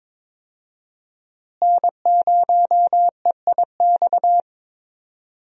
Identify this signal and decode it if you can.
Morse code, heard as a keyed tone, about 22 wpm, 700 Hz. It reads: N0EIX
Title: USA Call Signs - 5 Characters - Repeated 3x - 22wpm